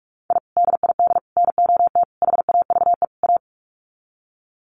Morse code, heard as a keyed tone, about 44 wpm, 700 Hz. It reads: I DID NOT HAVE A
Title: Sets of 5 Words 45wpm